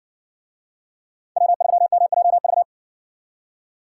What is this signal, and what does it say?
Morse code, heard as a keyed tone, about 57 wpm, 700 Hz. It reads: W3MJV